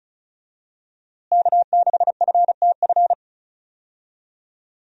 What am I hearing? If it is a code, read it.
Morse code, heard as a keyed tone, about 36 wpm, 700 Hz. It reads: K6FTF
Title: USA Call Signs - 5 Characters - Repeated 3x - 35wpm